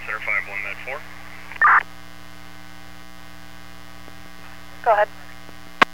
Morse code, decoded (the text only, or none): none